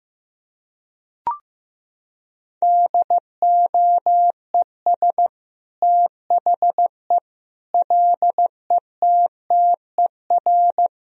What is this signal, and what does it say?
Morse code, heard as a keyed tone, about 15 wpm, 700 Hz. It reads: DOES THE LETTER
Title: Sentences from Top 1000 Words - Review Part 2 - 15wpm